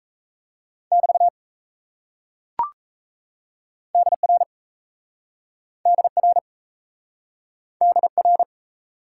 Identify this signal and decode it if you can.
Morse code, heard as a keyed tone, about 37 wpm, 700 Hz. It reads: <BT> DR DR DR